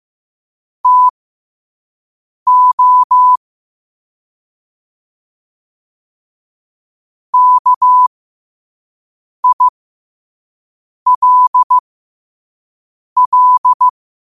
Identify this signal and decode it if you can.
Morse code, heard as a keyed tone, about 15 wpm, spaced as for 5 wpm, 1000 Hz. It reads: TO KILL
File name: qotdfiles / 5wpm